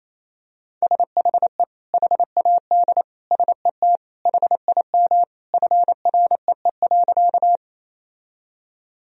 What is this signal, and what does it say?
Morse code, heard as a keyed tone, about 28 wpm, 700 Hz. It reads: SHE HAD SET HIM FREE.
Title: Sentences from Top 1000 Words - Review Part 1 - 28wpm